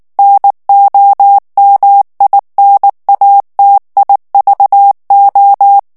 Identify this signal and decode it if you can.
Morse code, heard as a keyed tone, about 19 wpm, 800 Hz. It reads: NOMINATIVO
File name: morse mp3 / ITALIEN texte clair 2 vitesse 18 mots minute